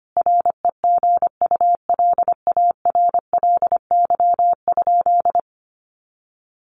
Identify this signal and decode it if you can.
Morse code, heard as a keyed tone, about 25 wpm, 700 Hz. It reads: REGULARLY?
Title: Sentences from Top 1400 Words 25wpm